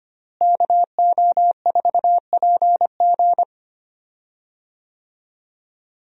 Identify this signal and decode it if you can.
Morse code, heard as a keyed tone, about 25 wpm, 700 Hz. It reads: KO4PG
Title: USA Call Signs - 5 Characters - Repeated 3x - 25wpm